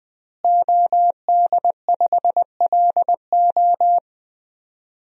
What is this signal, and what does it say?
Morse code, heard as a keyed tone, about 21 wpm, 700 Hz. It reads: OD5LO
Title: DXCC-100 Call Signs - 5 Characters - Repeated 3x - 20wpm